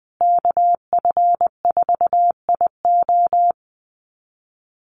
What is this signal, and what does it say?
Morse code, heard as a keyed tone, about 20 wpm, 700 Hz. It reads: KF4IO
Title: USA Call Signs Repeated 2x 20wpm